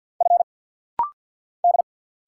Morse code, heard as a keyed tone, about 50 wpm, 690 Hz. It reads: F D
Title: Single Letter-Number Alt - Rapid-Fire 50wpm